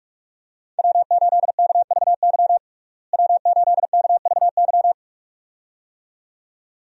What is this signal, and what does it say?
Morse code, heard as a keyed tone, about 47 wpm, 700 Hz. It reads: W8KVY W8KVY
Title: USA Call Signs - 5 Characters - Repeated 2x - 45wpm